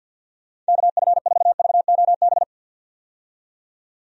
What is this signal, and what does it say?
Morse code, heard as a keyed tone, about 51 wpm, 700 Hz. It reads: KF4VOB
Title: USA Call Signs - 6 Characters - 50wpm